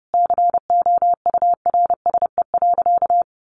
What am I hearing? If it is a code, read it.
Morse code, heard as a keyed tone, about 30 wpm, 700 Hz. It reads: COURSE.
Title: Sentences from Top 1500 Words 30wpm